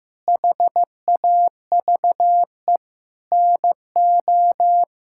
Morse code, heard as a keyed tone, about 15 wpm, 700 Hz. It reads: HAVE NO
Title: Sentences from Top 1000 Words - Review Part 5 - 15wpm